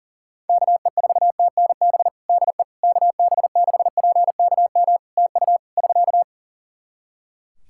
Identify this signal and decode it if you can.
Morse code, heard as a keyed tone, about 43 wpm, 700 Hz. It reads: KE4TNB DE KB6PKM TU <SK>
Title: Practice QSO - Exchange 4 - Ending - 40wpm